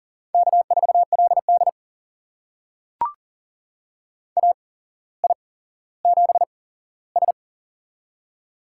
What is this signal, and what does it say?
Morse code, heard as a keyed tone, about 42 wpm, spaced as for 17 wpm, 700 Hz. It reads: K4LD AI7S
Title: USA Call Signs - 4 Characters - 8x Character Spacing - 40wpm